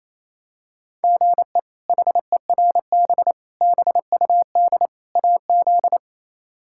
Morse code, heard as a keyed tone, about 27 wpm, 700 Hz. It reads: GE HERB BUD AZ
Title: K1USN SST QSO Practice - 28wpm